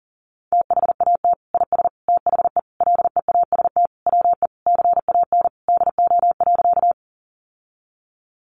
Morse code, heard as a keyed tone, about 40 wpm, 700 Hz. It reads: THAT IS THE LEAST WE CAN DO.
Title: Sentences from Top 700 Words 40wpm